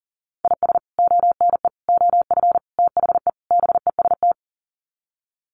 Morse code, heard as a keyed tone, about 40 wpm, 700 Hz. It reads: IS ONE OF THE BEST
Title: Sets of 5 Words 40wpm